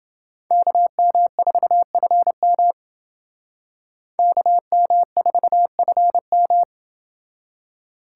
Morse code, heard as a keyed tone, about 29 wpm, 700 Hz. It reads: KM4FM KM4FM